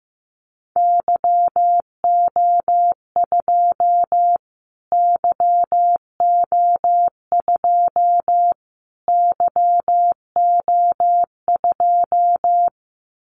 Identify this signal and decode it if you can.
Morse code, heard as a keyed tone, about 15 wpm, 700 Hz. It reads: YO2 YO2 YO2
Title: Callsign Prefixes for Top 25 DXCC - Repeated 3x - 15wpm